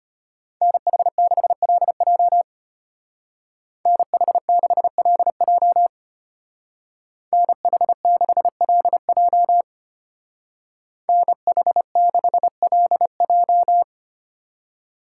Morse code, heard as a keyed tone, about 33 wpm, 700 Hz. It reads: NH6LJ NH6LJ NH6LJ NH6LJ